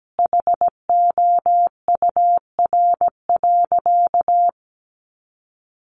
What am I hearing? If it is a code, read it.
Morse code, heard as a keyed tone, about 17 wpm, 700 Hz. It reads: HOUR.
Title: Sentences from Top 1100 Words 17wpm